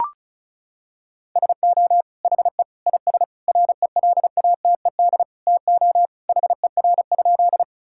Morse code, heard as a keyed tone, about 36 wpm, 700 Hz. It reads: SO HE IS RELATED TO HER?
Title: Sentences from Top 1000 Words - Review Part 5 - 35wpm